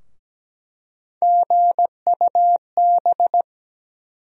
Morse code, heard as a keyed tone, about 17 wpm, 700 Hz. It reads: GUB
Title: USA Call Sign Postfixes - 3 Letters 17wpm